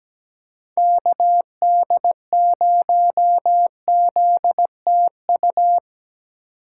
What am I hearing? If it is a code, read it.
Morse code, heard as a keyed tone, about 17 wpm, 700 Hz. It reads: KD0ZTU
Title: USA Call Signs - 6 Characters - 3x Character Spacing - 17wpm